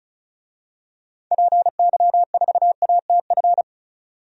Morse code, heard as a keyed tone, about 34 wpm, 700 Hz. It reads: PY4ATF
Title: DXCC-100 Call Signs - 6 Characters - 35wpm